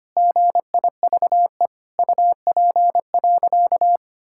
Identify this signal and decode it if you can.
Morse code, heard as a keyed tone, about 25 wpm, 700 Hz. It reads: GIVE UP.
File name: practice / 25wpm / Sentences with 3-Word Phrases - 25wpm